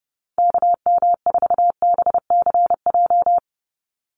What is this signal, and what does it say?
Morse code, heard as a keyed tone, about 30 wpm, 700 Hz. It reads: KM4BCJ
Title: USA Call Signs 30wpm